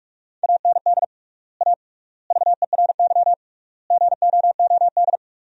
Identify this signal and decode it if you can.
Morse code, heard as a keyed tone, about 46 wpm, 700 Hz. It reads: AND A VERY GOOD
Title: 3x Word-Spacing - Sentences from Top 200 Words - 45wpm